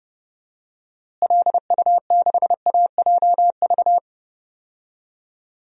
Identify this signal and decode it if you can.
Morse code, heard as a keyed tone, about 30 wpm, 700 Hz. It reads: LU6AJV
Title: DXCC-100 Call Signs - 6 Characters - Repeated 2x - 30wpm